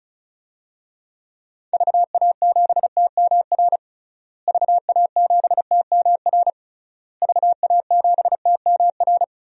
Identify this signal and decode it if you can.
Morse code, heard as a keyed tone, about 35 wpm, 700 Hz. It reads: VA7TMR VA7TMR VA7TMR
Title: DXCC-100 Call Signs - 6 Characters - Repeated 3x - 35wpm